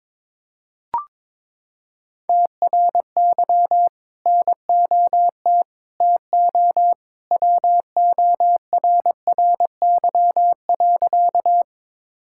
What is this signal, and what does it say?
Morse code, heard as a keyed tone, about 22 wpm, 700 Hz. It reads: TRY NOT TO WORRY.
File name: practice / 22wpm / Sentences with 2-Word Phrases - 22wpm